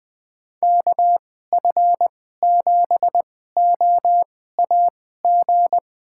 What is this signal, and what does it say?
Morse code, heard as a keyed tone, about 20 wpm, spaced as for 15 wpm, 700 Hz. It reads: KF7OAG
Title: USA Call Signs - 6 Characters - 2x Character Spacing - 20wpm